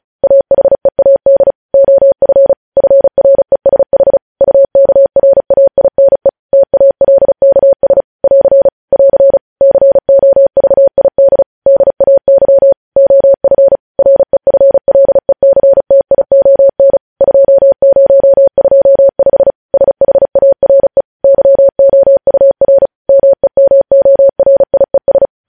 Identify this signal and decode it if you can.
Morse code, heard as a keyed tone, about 36 wpm, 550 Hz. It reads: AHEAD OF FRESH UKRAINE TALKS <AR> <AR> COVID DAY OF REFLECTION 2025 SHARE YOUR MEMORIES
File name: podcast / News headlines-1x-35x35